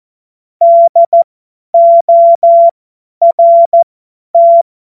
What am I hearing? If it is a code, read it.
Morse code, heard as a keyed tone, about 14 wpm, spaced as for 10 wpm, 680 Hz. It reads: DORT
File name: Morsekurs / Witze / Witz 23. Dezember 2025 50 F70